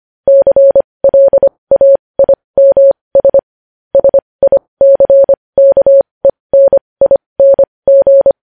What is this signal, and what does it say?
Morse code, heard as a keyed tone, about 25 wpm, spaced as for 20 wpm, 550 Hz. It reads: CLAIMS SICKENING
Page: news-headlines-in-morse-code-at-20-wpm